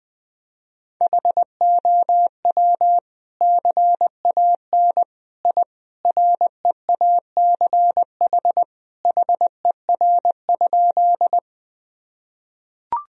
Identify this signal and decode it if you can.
Morse code, heard as a keyed tone, about 20 wpm, 700 Hz. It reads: HOW CAN I REACH HER?
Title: Sentences from Top 1000 Words - Review Part 2 - 20wpm